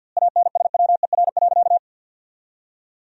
Morse code, heard as a keyed tone, about 50 wpm, 700 Hz. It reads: ANSWER.
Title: 2x Word-Spacing - Sentences from Top 500 Words - 50wpm